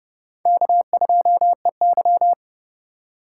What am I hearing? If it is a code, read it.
Morse code, heard as a keyed tone, about 28 wpm, 700 Hz. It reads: K2EY